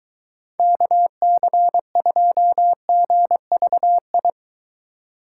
Morse code, heard as a keyed tone, about 22 wpm, 700 Hz. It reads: KC2GVI